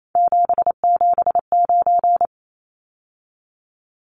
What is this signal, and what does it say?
Morse code, heard as a keyed tone, about 28 wpm, 700 Hz. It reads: 779
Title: Numbers - 3-Digit 28wpm